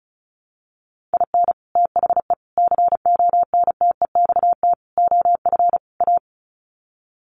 Morse code, heard as a keyed tone, about 34 wpm, 700 Hz. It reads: IN THE CONTEXT OF A
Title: Sets of 5 Words Encore 35wpm